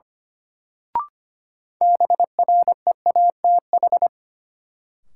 Morse code, heard as a keyed tone, about 25 wpm, 700 Hz. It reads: BREATH
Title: Sets of 5 Words Encore - New Word Drill - 25wpm